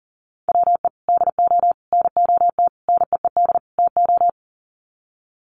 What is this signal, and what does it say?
Morse code, heard as a keyed tone, about 40 wpm, 700 Hz. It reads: WE DO NOT NEED TO
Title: Sets of 5 Words Encore 40wpm